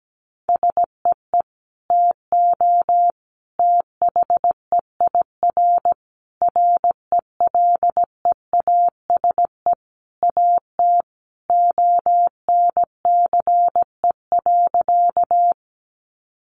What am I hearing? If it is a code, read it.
Morse code, heard as a keyed tone, about 17 wpm, 700 Hz. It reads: SEE TO THEIR RELEASE AT ONCE.
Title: Sentences from Top 1100 Words 17wpm